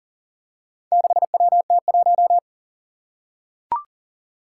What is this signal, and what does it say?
Morse code, heard as a keyed tone, about 41 wpm, 700 Hz. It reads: 6WT1
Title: Four Challenging Letter-Number Combinations 40wpm